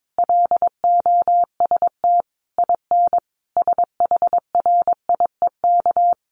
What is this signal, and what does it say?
Morse code, heard as a keyed tone, about 22 wpm, 700 Hz. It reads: LOST IN SHRIEK
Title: A Sound Of Thunder 22wpm